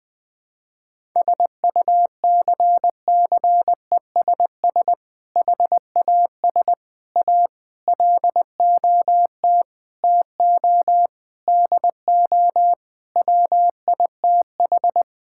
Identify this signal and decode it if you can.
Morse code, heard as a keyed tone, about 21 wpm, 700 Hz. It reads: SUCCESS HAS A LOT TO DO WITH
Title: Sentences with 6-Word Phrases - 20wpm